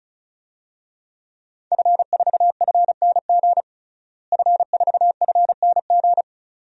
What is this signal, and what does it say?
Morse code, heard as a keyed tone, about 36 wpm, 700 Hz. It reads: F4FNG F4FNG
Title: DXCC-100 Call Signs - 5 Characters - Repeated 2x - 35wpm